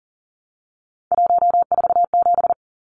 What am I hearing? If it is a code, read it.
Morse code, heard as a keyed tone, about 40 wpm, 700 Hz. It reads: 147
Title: Numbers - 3-Digit 40wpm